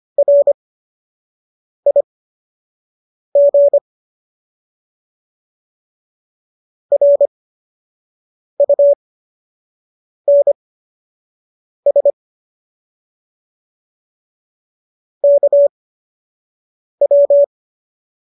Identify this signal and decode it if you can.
Morse code, heard as a keyed tone, about 25 wpm, spaced as for 6 wpm, 570 Hz. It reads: RIG RUNS KW